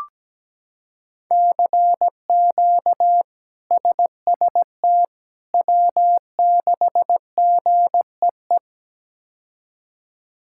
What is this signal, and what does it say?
Morse code, heard as a keyed tone, about 17 wpm, 700 Hz. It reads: CQ SST W6GEE
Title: K1USN SST QSO Practice - 17wpm